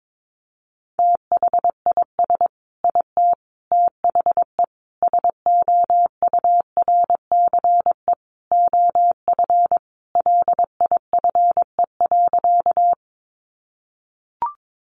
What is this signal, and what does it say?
Morse code, heard as a keyed tone, about 22 wpm, 700 Hz. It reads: THIS IT THE SOURCE OF LIFE.
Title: Sentences from Top 800 Words 22wpm